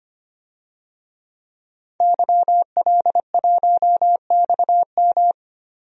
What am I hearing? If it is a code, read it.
Morse code, heard as a keyed tone, about 25 wpm, 700 Hz. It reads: YL1XM
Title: DXCC-100 Call Signs - 5 Characters - Repeated 2x - 25wpm